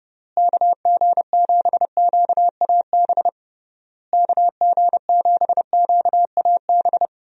Morse code, heard as a keyed tone, about 30 wpm, 700 Hz. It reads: KG7QAB KG7QAB
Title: USA Call Signs - 6 Characters - Repeated 3x - 30wpm